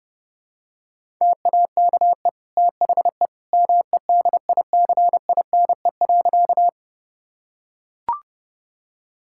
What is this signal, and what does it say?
Morse code, heard as a keyed tone, about 30 wpm, 700 Hz. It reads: TAKE THE MEDICINE.
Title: Sentences from Top 1000 Words - Review Part 6 - 30wpm